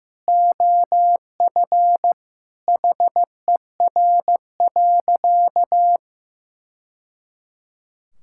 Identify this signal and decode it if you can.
Morse code, heard as a keyed tone, about 16 wpm, 700 Hz. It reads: OF HER.
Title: Sentences from Top 1000 Words - Review Part 5 - 15wpm